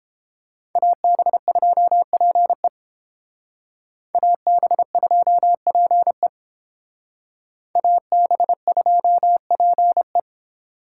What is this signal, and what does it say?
Morse code, heard as a keyed tone, about 28 wpm, 700 Hz. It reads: AB2PE AB2PE AB2PE